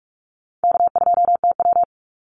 Morse code, heard as a keyed tone, about 44 wpm, 700 Hz. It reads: K2TW
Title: Easy Call Signs - 45wpm